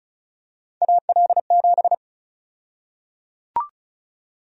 Morse code, heard as a keyed tone, about 34 wpm, 700 Hz. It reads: AL7
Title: USA Call Sign Prefixes - Repeated 2x - 35wpm